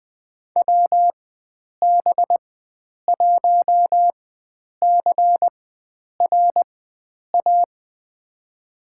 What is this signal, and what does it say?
Morse code, heard as a keyed tone, about 21 wpm, spaced as for 9 wpm, 700 Hz. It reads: WB1CRA
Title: USA Call Signs - 4x Character Spacing - 20wpm